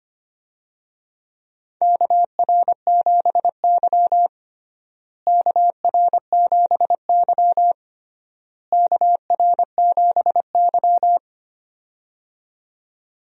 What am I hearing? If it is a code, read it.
Morse code, heard as a keyed tone, about 25 wpm, 700 Hz. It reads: KR7Y KR7Y KR7Y
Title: USA Call Signs - 4 Characters - Repeated 3x - 25wpm